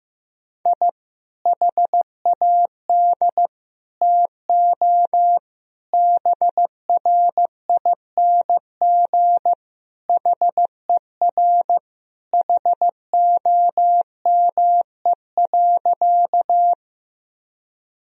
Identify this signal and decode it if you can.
Morse code, heard as a keyed tone, about 15 wpm, 700 Hz. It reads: I HAD TO BRING HER HOME.
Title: Sentences from Top 1000 Words - Review Part 3 - 15wpm